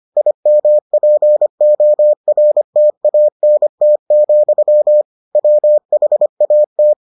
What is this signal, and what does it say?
Morse code, heard as a keyed tone, about 25 wpm, 600 Hz. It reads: IMPORTANT, WHAT